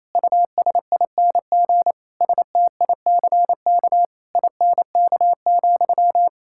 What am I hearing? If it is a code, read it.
Morse code, heard as a keyed tone, about 28 wpm, 700 Hz. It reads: USING STICK INK,
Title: Pangram Challenge v2 - 28wpm